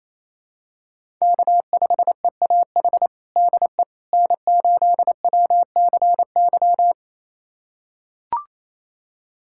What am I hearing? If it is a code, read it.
Morse code, heard as a keyed tone, about 28 wpm, 700 Hz. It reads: K5EAH DE N8WCY
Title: USA Callsign DE Callsign - 5 Characters - 28wpm